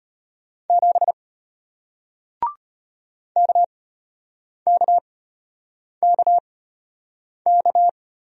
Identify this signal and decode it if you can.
Morse code, heard as a keyed tone, about 33 wpm, spaced as for 8 wpm, 700 Hz. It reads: 7 KKKK